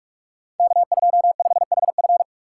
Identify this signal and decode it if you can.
Morse code, heard as a keyed tone, about 44 wpm, 700 Hz. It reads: K25HF
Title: Code Groups - 5 Letter:Numbers - 45wpm